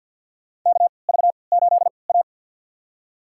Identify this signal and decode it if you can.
Morse code, heard as a keyed tone, about 53 wpm, spaced as for 29 wpm, 700 Hz. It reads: KV8A
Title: USA Call Signs - 4 Characters - 3x Character Spacing - 50wpm